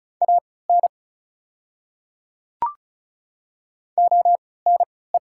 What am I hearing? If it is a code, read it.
Morse code, heard as a keyed tone, about 36 wpm, spaced as for 20 wpm, 690 Hz. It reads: AN ONE
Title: 3x Character Spacing - Top 100 Words - 35wpm